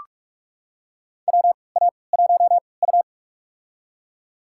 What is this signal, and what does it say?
Morse code, heard as a keyed tone, about 44 wpm, spaced as for 25 wpm, 690 Hz. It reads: WA1U
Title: USA Call Signs - 4 Characters - 3x Character Spacing - 45wpm